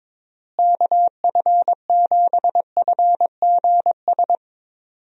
Morse code, heard as a keyed tone, about 22 wpm, 700 Hz. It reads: KF7FGS
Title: USA Call Signs - 6 Characters - Repeated 3x - 22wpm